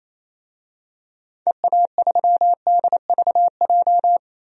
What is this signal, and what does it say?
Morse code, heard as a keyed tone, about 28 wpm, 700 Hz. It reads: EA3DVJ
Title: DXCC-100 Call Signs - 6 Characters - 28wpm